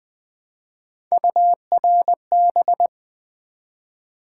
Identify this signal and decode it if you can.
Morse code, heard as a keyed tone, about 20 wpm, 700 Hz. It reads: URB
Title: Build A Word - 1 Letter At A Time - 5-Letter Words - 20wpm